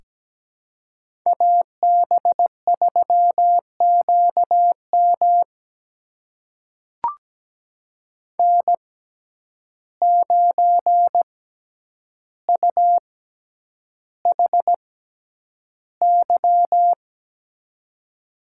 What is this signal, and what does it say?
Morse code, heard as a keyed tone, about 17 wpm, spaced as for 9 wpm, 700 Hz. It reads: AB3QM N9UHY
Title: USA Call Signs - 5 Characters - 6x Character Spacing - 17wpm